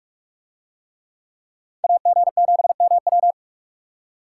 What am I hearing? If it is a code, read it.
Morse code, heard as a keyed tone, about 40 wpm, 700 Hz. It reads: AG7MW